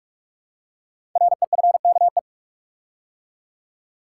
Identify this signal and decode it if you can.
Morse code, heard as a keyed tone, about 44 wpm, 700 Hz. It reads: REFKE
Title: Code Groups - 5 Letter:Numbers - 45wpm